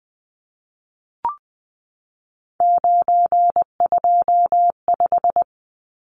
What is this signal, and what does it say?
Morse code, heard as a keyed tone, about 20 wpm, 700 Hz. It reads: 925
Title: Numbers - 3-Digit 20wpm